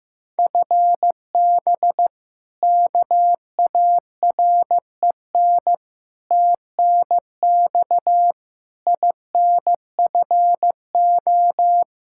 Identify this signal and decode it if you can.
Morse code, heard as a keyed tone, about 15 wpm, 700 Hz. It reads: FB KAREN TNX INFO
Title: K1USN SST QSO Practice 15wpm